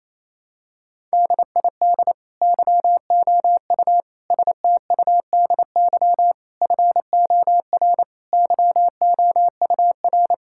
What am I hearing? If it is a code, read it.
Morse code, heard as a keyed tone, about 28 wpm, 700 Hz. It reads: DID YOU STUDY FOR YOUR
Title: Sentences from Top 1000 Words - Review Part 6 - 28wpm